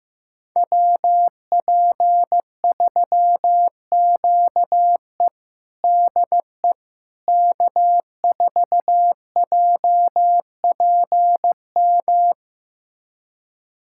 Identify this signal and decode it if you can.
Morse code, heard as a keyed tone, about 16 wpm, 700 Hz. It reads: WP3QE DE K4JPM
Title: USA Callsign DE Callsign - 5 Characters - 15wpm